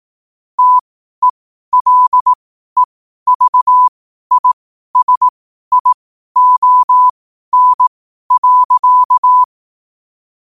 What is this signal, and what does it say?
Morse code, heard as a keyed tone, about 18 wpm, spaced as for 12 wpm, 1000 Hz. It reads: TELEVISION.
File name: qotdfiles / 12wpm